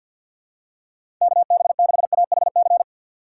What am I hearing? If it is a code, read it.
Morse code, heard as a keyed tone, about 50 wpm, 700 Hz. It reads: KB6AHC
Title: USA Call Signs - 6 Characters - 6x Character Spacing - 50wpm